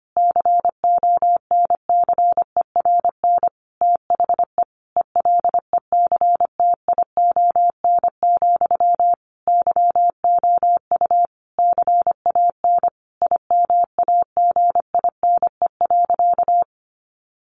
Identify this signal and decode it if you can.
Morse code, heard as a keyed tone, about 25 wpm, 700 Hz. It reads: CONCERN THE ELECTION, YOU CAN IMAGINE.
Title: Sentences from Top 1500 Words 25wpm